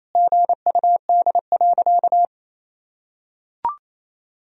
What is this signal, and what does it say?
Morse code, heard as a keyed tone, about 28 wpm, 700 Hz. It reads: GUD.
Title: Sentences from Top 200 Words and Abbreviations 28wpm